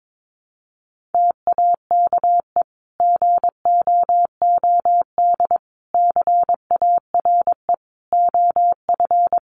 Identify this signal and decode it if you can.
Morse code, heard as a keyed tone, about 22 wpm, 700 Hz. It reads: TAKE GOOD CARE OF
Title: Sentences from Top 400 Words 22wpm